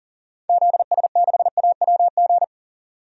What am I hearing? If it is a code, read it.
Morse code, heard as a keyed tone, about 40 wpm, 700 Hz. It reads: ZS6AWG
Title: DXCC-100 Call Signs - 6 Characters - 40wpm